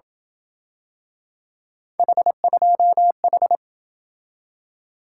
Morse code, heard as a keyed tone, about 27 wpm, 700 Hz. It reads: H2H